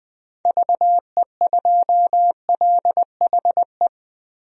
Morse code, heard as a keyed tone, about 21 wpm, 700 Hz. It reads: VE2LHE
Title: DXCC-100 Call Signs - 6 Characters - 20wpm